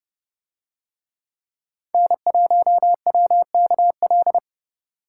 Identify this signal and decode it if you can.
Morse code, heard as a keyed tone, about 29 wpm, 700 Hz. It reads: N1WKL